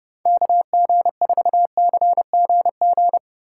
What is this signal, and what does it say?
Morse code, heard as a keyed tone, about 30 wpm, 700 Hz. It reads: KG4CGG
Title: USA Call Signs - 6 Characters - 4x Character Spacing - 30wpm